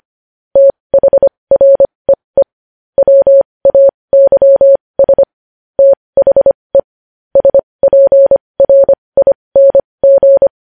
Morse code, heard as a keyed tone, about 25 wpm, spaced as for 20 wpm, 550 Hz. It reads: THREE WAYS THE SPRING
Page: News Headlines in Morse Code at 20 WPM